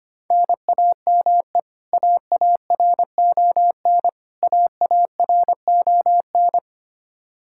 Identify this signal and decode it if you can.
Morse code, heard as a keyed tone, about 25 wpm, 700 Hz. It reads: NAME AARON AARON
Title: Practice QSO - Exchange 1 - Beginning - 25wpm